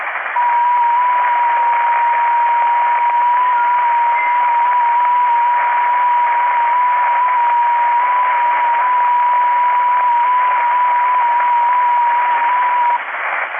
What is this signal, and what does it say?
FT8.
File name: contests / 2019-june=vhf-k5tr- -0101 0-22